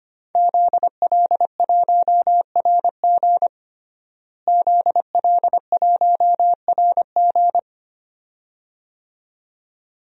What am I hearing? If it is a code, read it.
Morse code, heard as a keyed tone, about 25 wpm, 700 Hz. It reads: ZL1RG ZL1RG
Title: DXCC-100 Call Signs - 5 Characters - Repeated 2x - 25wpm